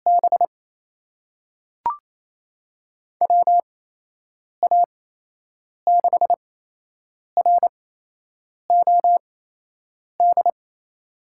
Morse code, heard as a keyed tone, about 28 wpm, spaced as for 8 wpm, 700 Hz. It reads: B WA6ROD
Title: USA Call Signs - 6 Characters - 8x Character Spacing - 28wpm